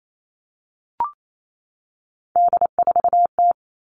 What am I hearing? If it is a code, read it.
Morse code, heard as a keyed tone, about 28 wpm, 700 Hz. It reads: D4T
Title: All Cut Numbers - 3-Digit 28wpm